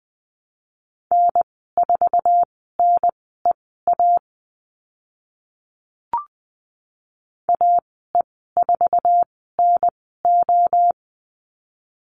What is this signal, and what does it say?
Morse code, heard as a keyed tone, about 20 wpm, spaced as for 14 wpm, 700 Hz. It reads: N4NEA AE4NO
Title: 048 Lesson 9g - Number 4 - Call Signs 20wpm